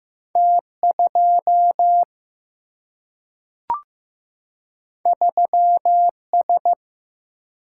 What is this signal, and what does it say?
Morse code, heard as a keyed tone, about 15 wpm, 700 Hz. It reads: T2 3S
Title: Single-Letter Single-Number 15wpm